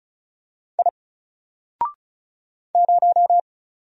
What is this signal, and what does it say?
Morse code, heard as a keyed tone, about 35 wpm, 700 Hz. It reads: I 0
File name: Single-Letter-Number-35wpm